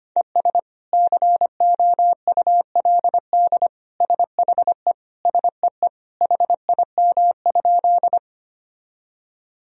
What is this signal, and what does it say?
Morse code, heard as a keyed tone, about 25 wpm, 700 Hz. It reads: ES COULD SHE SEE HIM?
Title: Sentences from Top 100 Words and Abbreviations 25wpm